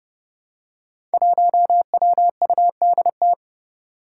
Morse code, heard as a keyed tone, about 30 wpm, 700 Hz. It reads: 1WUDT
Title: Five Challenging Letter-Number Combinations 30wpm